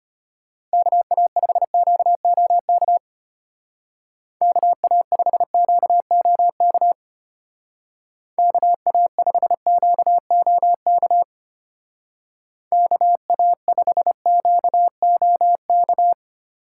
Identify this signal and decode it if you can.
Morse code, heard as a keyed tone, about 31 wpm, 700 Hz. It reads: KA5QOK KA5QOK KA5QOK KA5QOK
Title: Speed Racing - USA 2x3 Call Signs 25wpm